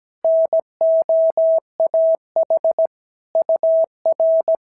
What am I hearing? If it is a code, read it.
Morse code, heard as a keyed tone, about 17 wpm, 650 Hz. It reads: NOAH UR
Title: SOTA QSO Practice 17wpm